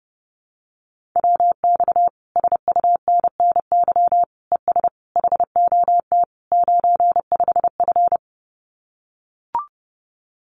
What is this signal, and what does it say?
Morse code, heard as a keyed tone, about 30 wpm, 700 Hz. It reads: WX SUNNY ES HOT 95F
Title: Common QSO Phrases Encore 30wpm